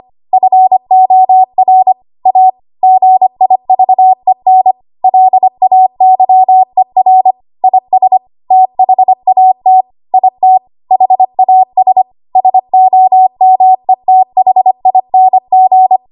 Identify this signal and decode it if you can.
Morse code, heard as a keyed tone, about 25 wpm, 750 Hz. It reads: FOR A GIVEN LAYER IS THAT IT HAS SOMETHING
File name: Archive / 25 WPM / 25WPM